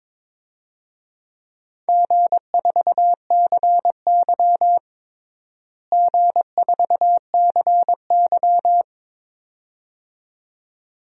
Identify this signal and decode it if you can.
Morse code, heard as a keyed tone, about 22 wpm, 700 Hz. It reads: G4CY G4CY
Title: DXCC-100 Call Signs - 4 Characters - Repeated 2x - 22wpm